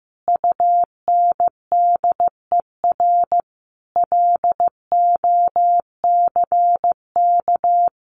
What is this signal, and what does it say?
Morse code, heard as a keyed tone, about 15 wpm, 700 Hz. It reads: UNDER LOCK
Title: Sentences from Top 1400 Words 15wpm